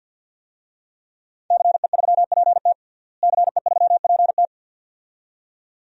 Morse code, heard as a keyed tone, about 46 wpm, 700 Hz. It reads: CE3PT CE3PT
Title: DXCC-100 Call Signs - 5 Characters - Repeated 2x - 50wpm